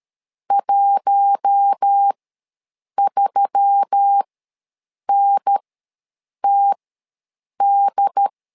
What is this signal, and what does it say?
Morse code, heard as a keyed tone, about 13 wpm, spaced as for 7 wpm, 800 Hz. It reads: 13NTD
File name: LEZIONI / 4 Q1S3 ws 12wpm 2038